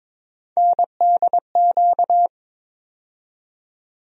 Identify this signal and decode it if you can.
Morse code, heard as a keyed tone, about 22 wpm, 700 Hz. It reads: NDQ
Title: USA Call Sign Postfixes - 3 Letters 22wpm